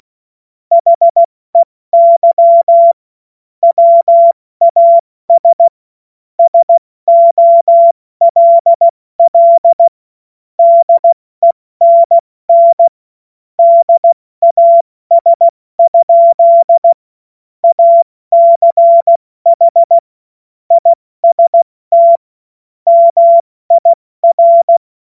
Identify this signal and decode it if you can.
Morse code, heard as a keyed tone, about 16 wpm, 680 Hz. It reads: HEY WAS SOLL DENN DAS? ACH IST MIR
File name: Morsekurs / Witze / Witz 22. Februar 2022 70 F80